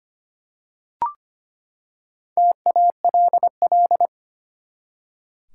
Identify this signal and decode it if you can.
Morse code, heard as a keyed tone, about 25 wpm, 700 Hz. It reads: TALL
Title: Four Letter Words - 25wpm